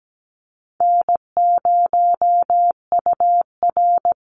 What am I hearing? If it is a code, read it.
Morse code, heard as a keyed tone, about 17 wpm, 700 Hz. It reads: N0UR
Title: Call Signs Repeated 3x - 17wpm